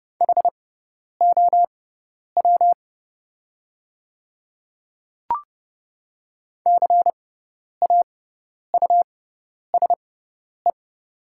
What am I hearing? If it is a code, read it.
Morse code, heard as a keyed tone, about 30 wpm, spaced as for 10 wpm, 700 Hz. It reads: HOW CAUSE
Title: 6x Character Spacing - Top 200 Words - 30wpm